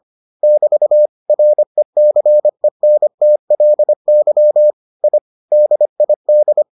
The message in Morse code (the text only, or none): <BT> RECENTLY I DID